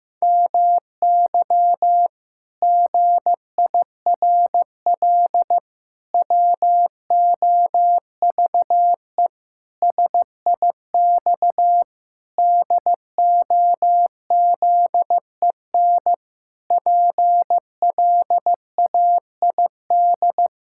Morse code, heard as a keyed tone, about 16 wpm, 700 Hz. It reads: MY GIRL WOVE SIX DOZEN PLAID
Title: Pangram Challenge v2 - 15wpm